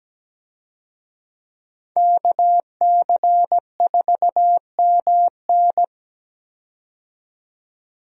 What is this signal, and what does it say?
Morse code, heard as a keyed tone, about 17 wpm, 700 Hz. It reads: KC4MN
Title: USA Call Signs - 5 Characters - 17wpm